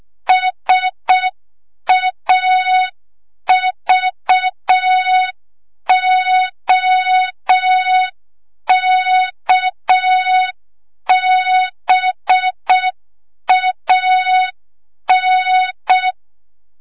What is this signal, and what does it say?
Morse code, heard as a keyed tone, about 6 wpm, 3700 Hz. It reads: SAVOKBAN
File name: Morze-HA2MN / hang / 25 GYAKORLAT